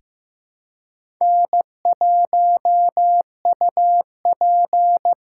Morse code, heard as a keyed tone, about 15 wpm, 700 Hz. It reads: N1UP
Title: USA Call Signs - 4 Characters - 2x Character Spacing - 15wpm